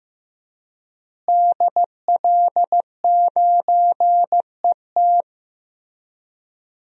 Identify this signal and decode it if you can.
Morse code, heard as a keyed tone, about 15 wpm, 700 Hz. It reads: DL9ET
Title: DXCC-100 Call Signs - 5 Characters - Repeated 3x - 15wpm